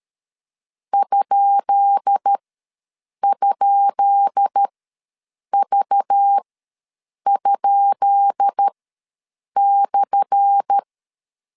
Morse code, heard as a keyed tone, about 13 wpm, spaced as for 7 wpm, 800 Hz. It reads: ??V?/
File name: LEZIONI / 10 SIGNS / 10B INT ws 12wpm 3527